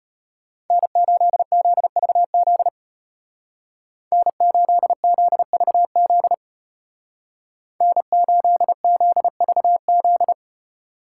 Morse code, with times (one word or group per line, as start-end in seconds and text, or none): 0.7-2.7 N8ZVZ
4.1-6.3 N8ZVZ
7.8-10.3 N8ZVZ